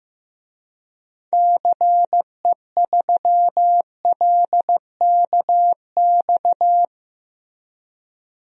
Morse code, heard as a keyed tone, about 15 wpm, 700 Hz. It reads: CE3LKX
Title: DXCC-100 Call Signs - 6 Characters - Repeated 2x - 15wpm